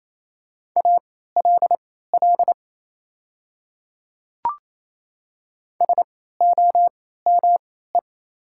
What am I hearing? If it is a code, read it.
Morse code, heard as a keyed tone, about 27 wpm, spaced as for 16 wpm, 700 Hz. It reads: ALL SOME
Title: 3x Character Spacing - Top 100 Words - 28wpm